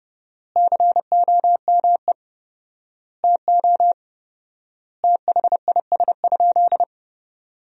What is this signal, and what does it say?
Morse code, heard as a keyed tone, about 30 wpm, 700 Hz. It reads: COME TO THIS?
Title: 4x Word-Spacing - Sentences from Top 100 Words - 30wpm